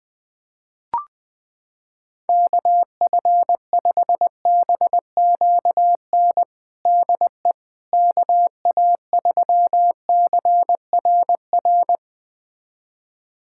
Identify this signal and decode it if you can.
Morse code, heard as a keyed tone, about 20 wpm, 700 Hz. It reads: KF5BQN DE KA3CRR
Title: USA Callsign DE Callsign - 6 Characters - 20wpm